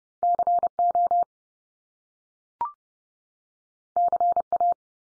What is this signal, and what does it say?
Morse code, heard as a keyed tone, about 30 wpm, 700 Hz. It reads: CO CA
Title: US States 30WPM